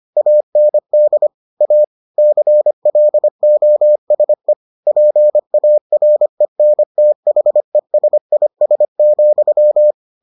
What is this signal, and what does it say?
Morse code, heard as a keyed tone, about 25 wpm, 600 Hz. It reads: AND A CLOSE PARENTHESIS,